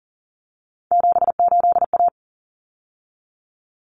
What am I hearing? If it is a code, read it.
Morse code, heard as a keyed tone, about 40 wpm, 700 Hz. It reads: 78A
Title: Common Cut Numbers - 3-Digit 40wpm